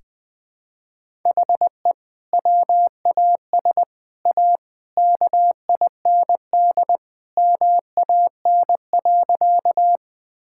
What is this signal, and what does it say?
Morse code, heard as a keyed tone, about 20 wpm, 700 Hz. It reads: HE WAS A KIND MAN.
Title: Sentences with 3-Word Phrases - Spoken Beginning - 20wpm